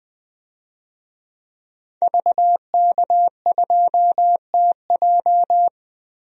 Morse code, heard as a keyed tone, about 20 wpm, 700 Hz. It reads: VK2TJ
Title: DXCC-100 Call Signs - 5 Characters - Repeated 2x - 20wpm